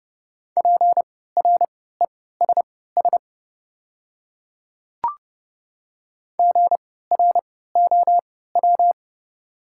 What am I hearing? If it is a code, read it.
Morse code, heard as a keyed tone, about 30 wpm, spaced as for 17 wpm, 700 Hz. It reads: PRESS GROW
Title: 3x Character Spacing - Top 300 Words - 30wpm